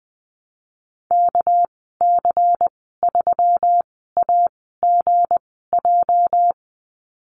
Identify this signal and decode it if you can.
Morse code, heard as a keyed tone, about 20 wpm, spaced as for 14 wpm, 700 Hz. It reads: KC3AGJ
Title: 213 Lesson 34g - Number 8 - Call Signs 20wpm